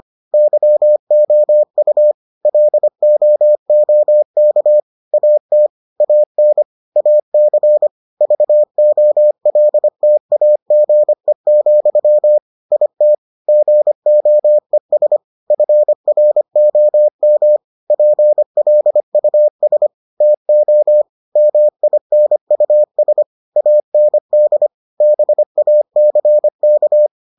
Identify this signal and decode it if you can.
Morse code, heard as a keyed tone, about 25 wpm, 600 Hz. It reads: YOU LOOK AT AN AC VOLTAGE, IT GOES FROM PLUS TO MINUS AND BACK